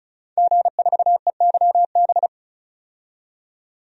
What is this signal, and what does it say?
Morse code, heard as a keyed tone, about 35 wpm, 700 Hz. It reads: G4EYB
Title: DXCC-100 Call Signs - 5 Characters - 35wpm